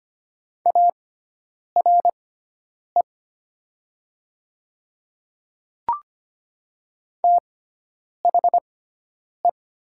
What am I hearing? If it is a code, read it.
Morse code, heard as a keyed tone, about 25 wpm, spaced as for 9 wpm, 700 Hz. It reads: ARE THE
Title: 6x Character Spacing - Top 500 Words - 25wpm